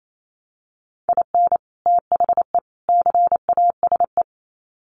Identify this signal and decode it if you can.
Morse code, heard as a keyed tone, about 28 wpm, 700 Hz. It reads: IN THE CASE
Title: Sets of 3 Words 28wpm